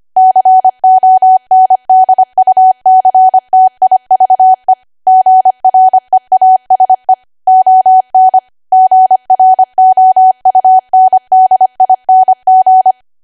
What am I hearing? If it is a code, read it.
Morse code, heard as a keyed tone, about 25 wpm, 750 Hz. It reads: CONDUCTIVE GREASE ON GROUNDING
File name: Archive / 25 WPM / 25WPM